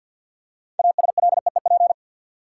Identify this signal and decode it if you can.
Morse code, heard as a keyed tone, about 48 wpm, 700 Hz. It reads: ASLEEP
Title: Six Letter Words - 50wpm